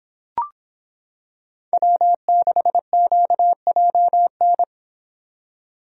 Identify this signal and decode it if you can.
Morse code, heard as a keyed tone, about 25 wpm, 700 Hz. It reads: W6QJN